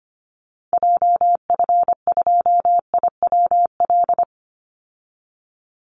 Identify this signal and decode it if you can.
Morse code, heard as a keyed tone, about 25 wpm, 700 Hz. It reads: JF2IWL
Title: Call Signs Repeated 3x - 25wpm